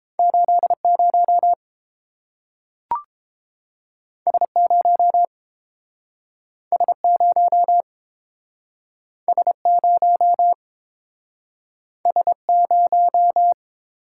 Morse code, heard as a keyed tone, about 30 wpm, 700 Hz. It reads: 80 S0 S0 S0 S0